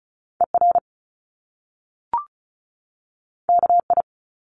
Morse code, heard as a keyed tone, about 36 wpm, 700 Hz. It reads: ER KI
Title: Top Two Letter Combinations in Words 35wpm